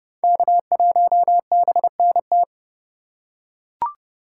Morse code, heard as a keyed tone, about 30 wpm, 700 Hz. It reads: K1BNT
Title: USA Call Signs - 5 Characters - 4x Character Spacing - 30wpm